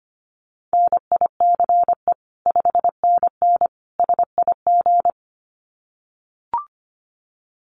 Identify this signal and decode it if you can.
Morse code, heard as a keyed tone, about 25 wpm, 700 Hz. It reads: NICE 5NN SIG
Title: Common QSO Phrases 25wpm